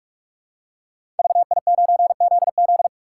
Morse code, heard as a keyed tone, about 44 wpm, 700 Hz. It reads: VI9ZZ
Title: Code Groups - 5 Letter:Numbers - 45wpm